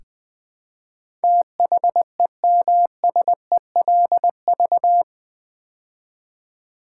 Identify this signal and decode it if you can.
Morse code, heard as a keyed tone, about 20 wpm, 700 Hz. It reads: THEMSELV
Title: Build A Word - 4 Letters At A Time - 10-Letter Words - 20wpm